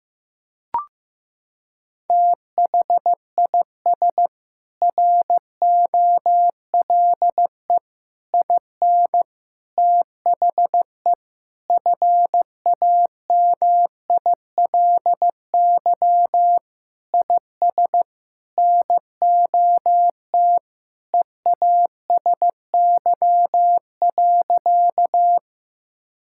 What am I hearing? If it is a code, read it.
Morse code, heard as a keyed tone, about 15 wpm, 700 Hz. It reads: THIS ROLE IN THE FAMILY IS NOT EASY.
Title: Sentences from Top 1000 Words - Review Part 6 - 15wpm